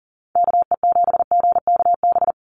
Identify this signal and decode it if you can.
Morse code, heard as a keyed tone, about 42 wpm, 700 Hz. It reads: KE7GKB
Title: USA Call Signs 40wpm